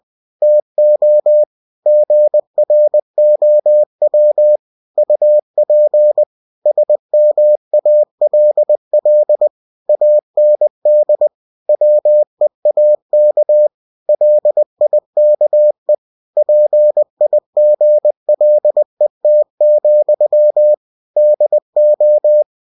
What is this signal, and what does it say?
Morse code, heard as a keyed tone, about 20 wpm, 600 Hz. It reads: TO GROW UP SMALL AND WEAK LIKE PIGLET, DO
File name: cw-books / winnie-the-pooh / winnie-20wpm-page-0073